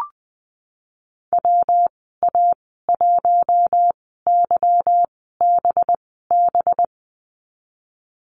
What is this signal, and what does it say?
Morse code, heard as a keyed tone, about 20 wpm, spaced as for 15 wpm, 700 Hz. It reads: WA1YBB
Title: 193 Lesson 31g - Letter v - Call Signs 20wpm